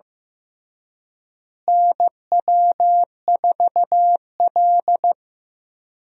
Morse code, heard as a keyed tone, about 15 wpm, 700 Hz. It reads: NW4L
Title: USA Call Signs - 4 Characters - Repeated 3x - 15wpm